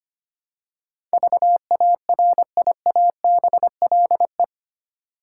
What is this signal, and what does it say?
Morse code, heard as a keyed tone, about 25 wpm, 690 Hz. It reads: VARIABLE
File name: 8 Letter Words - First 3 Letters Given - 25wpm